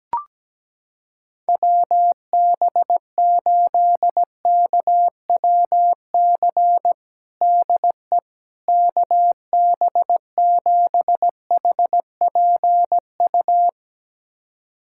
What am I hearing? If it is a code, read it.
Morse code, heard as a keyed tone, about 17 wpm, 700 Hz. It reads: WB8KWC DE KB7HPU
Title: USA Callsign DE Callsign - 6 Characters - 17wpm